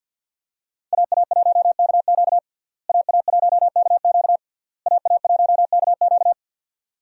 Morse code, heard as a keyed tone, about 52 wpm, 700 Hz. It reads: AA1KQ AA1KQ AA1KQ
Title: USA Call Signs - 5 Characters - Repeated 3x - 50wpm